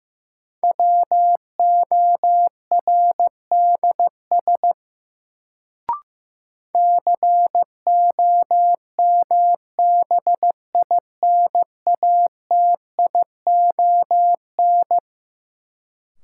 Morse code, heard as a keyed tone, about 16 wpm, 700 Hz. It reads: WORDS COMBINATION
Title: Sentences from Top 1000 Words - Review Part 6 - New Word Drill - 15wpm